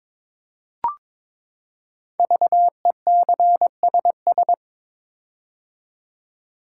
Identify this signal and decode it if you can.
Morse code, heard as a keyed tone, about 22 wpm, 700 Hz. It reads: VECSS
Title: Code Groups - 5 Letters - 22wpm